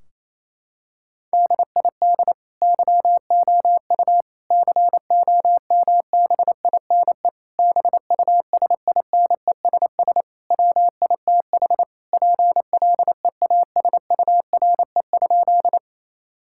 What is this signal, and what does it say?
Morse code, heard as a keyed tone, about 28 wpm, 700 Hz. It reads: DID YOU COMBINE BUSINESS WITH PLEASURE?
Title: Sentences from Top 1600 Words 28wpm